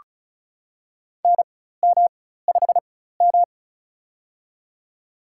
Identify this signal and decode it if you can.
Morse code, heard as a keyed tone, about 36 wpm, spaced as for 16 wpm, 700 Hz. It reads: NM5M
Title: USA Call Signs - 4 Characters - 4x Character Spacing - 35wpm